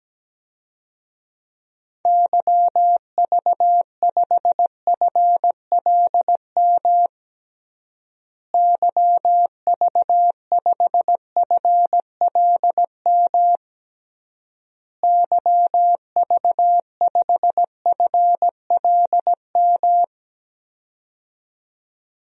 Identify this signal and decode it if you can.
Morse code, heard as a keyed tone, about 17 wpm, 700 Hz. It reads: YV5FLM YV5FLM YV5FLM
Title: DXCC-100 Call Signs - 6 Characters - Repeated 3x - 17wpm